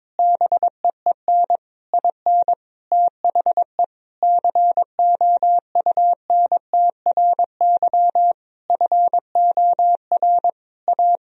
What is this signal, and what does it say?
Morse code, heard as a keyed tone, about 22 wpm, 700 Hz. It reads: BEEN IN THE COUNTRY FOR A
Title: Sentences from Top 1000 Words - Review Part 1 - 22wpm